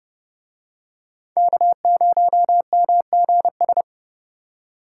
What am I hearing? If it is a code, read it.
Morse code, heard as a keyed tone, about 27 wpm, 700 Hz. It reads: K0MGS